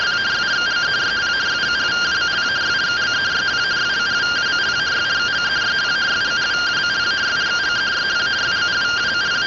Two-tone teleprinter signal, about 170 Hz shift.
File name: findings / CT 1 Scout 0001.gen-04 rtty rx